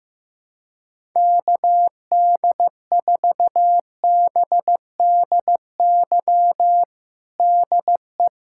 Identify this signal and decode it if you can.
Morse code, heard as a keyed tone, about 15 wpm, 700 Hz. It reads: KD4BDY DE
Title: USA Callsign DE Callsign 15wpm